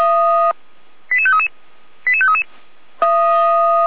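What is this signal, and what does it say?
Five-tone selective call (ZVEI), heard as quick runs of five tones, tones of about 70 ms. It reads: 88533 88533
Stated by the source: ZVEI five-tone selective call received by SDR, from the audio server of Feuerwehr Orlamünde fire brigade.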